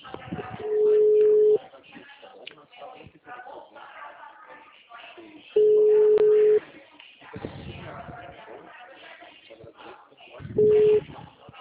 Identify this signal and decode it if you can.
Ringback tone (Europe, 425 Hz), with speech.